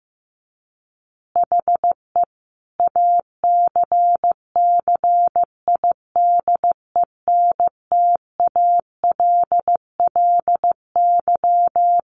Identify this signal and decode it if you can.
Morse code, heard as a keyed tone, about 15 wpm, 700 Hz. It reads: HE ACCIDENTALLY
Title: Additional Easy Phrases 15wpm